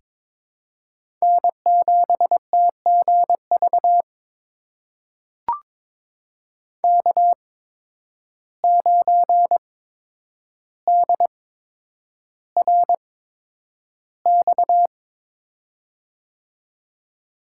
Morse code, heard as a keyed tone, about 22 wpm, spaced as for 9 wpm, 700 Hz. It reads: N7TGV K9DRX
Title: USA Call Signs - 5 Characters - 8x Character Spacing - 22wpm